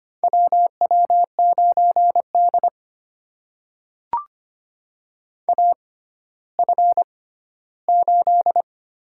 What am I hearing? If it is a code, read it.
Morse code, heard as a keyed tone, about 25 wpm, spaced as for 14 wpm, 700 Hz. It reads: WW9D AF8
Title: USA Call Signs - 4 Characters - 6x Character Spacing - 25wpm